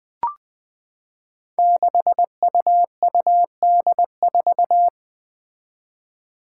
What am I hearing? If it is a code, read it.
Morse code, heard as a keyed tone, about 20 wpm, 700 Hz. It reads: 6UUD4
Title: Five Challenging Letter-Number Combinations 20wpm